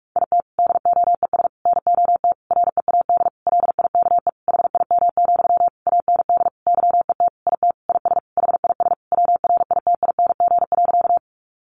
Morse code, heard as a keyed tone, about 44 wpm, 700 Hz. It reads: IT DOES NOT READ LIKE HIM, AND YET IT IS HIS WRITING.
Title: Sentences from Top 800 Words 45wpm